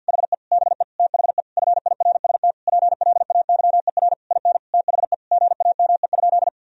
Morse code, heard as a keyed tone, about 50 wpm, 700 Hz. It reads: HE BE THE FIRST PLAYER IN THE GAME?
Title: Sentences from Top 1000 Words - Review Part 4 - 50wpm